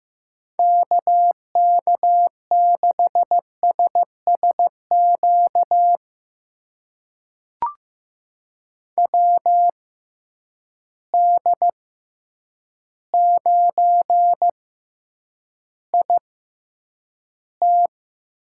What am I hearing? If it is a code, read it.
Morse code, heard as a keyed tone, about 15 wpm, spaced as for 8 wpm, 700 Hz. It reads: KK6SSQ WD9IT
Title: USA Call Signs - 6 Characters - 6x Character Spacing - 15wpm